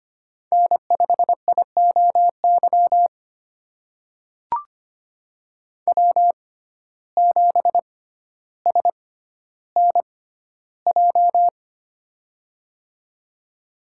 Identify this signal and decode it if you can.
Morse code, heard as a keyed tone, about 25 wpm, spaced as for 13 wpm, 700 Hz. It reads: N5IOY W7SNJ
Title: USA Call Signs - 5 Characters - 6x Character Spacing - 25wpm